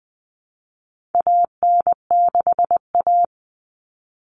Morse code, heard as a keyed tone, about 20 wpm, 700 Hz. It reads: AN6A
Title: All Cut Numbers - 4-Digit 20wpm